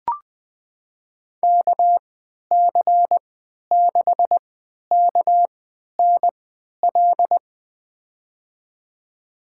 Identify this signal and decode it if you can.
Morse code, heard as a keyed tone, about 20 wpm, spaced as for 11 wpm, 700 Hz. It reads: KC6KNL
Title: USA Call Signs - 6 Characters - 3x Character Spacing - 20wpm